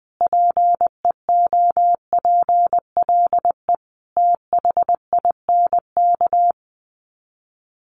Morse code, heard as a keyed tone, about 20 wpm, 700 Hz. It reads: PEOPLE THINK
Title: Sets of 5 Words Encore 20wpm